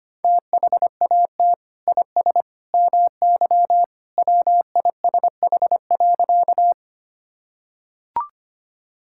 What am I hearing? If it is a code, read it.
Morse code, heard as a keyed tone, about 25 wpm, 700 Hz. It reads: THAT IS MY WISH.
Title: Sentences from Top 1600 Words 25wpm